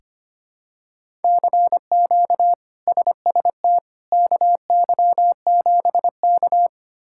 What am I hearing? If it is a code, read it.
Morse code, heard as a keyed tone, about 25 wpm, 700 Hz. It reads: CQ SST KY7K
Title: K1USN SST QSO Practice - 25wpm